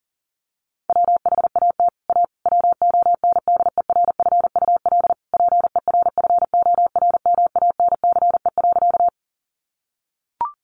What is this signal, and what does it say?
Morse code, heard as a keyed tone, about 40 wpm, 700 Hz. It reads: WHAT A WONDERFUL PERFORMANCE.
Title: Sentences from Top 1100 Words 40wpm